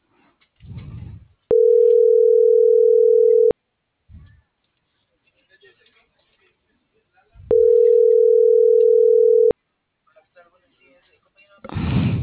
Ringback tone (North America), 440 + 480 Hz.